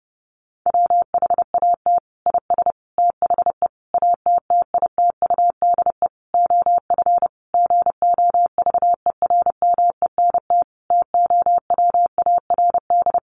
Morse code, heard as a keyed tone, about 30 wpm, 700 Hz. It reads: WHAT IS THE ATTITUDE OF GOVERMENT TOWARD
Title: Sentences from Top 1300 Words 30wpm